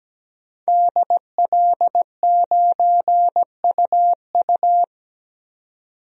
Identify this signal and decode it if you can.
Morse code, heard as a keyed tone, about 17 wpm, 700 Hz. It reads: DL9UU
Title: DXCC-100 Call Signs - 5 Characters - 17wpm